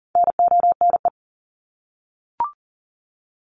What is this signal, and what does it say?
Morse code, heard as a keyed tone, about 40 wpm, 700 Hz. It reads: NONE
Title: Top 1501-1750 Words 40wpm